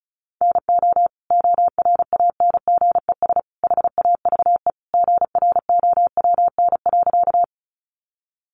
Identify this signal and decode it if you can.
Morse code, heard as a keyed tone, about 34 wpm, 700 Hz. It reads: NO ORANGES HAVE GROWN.
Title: Easy Phrases 35wpm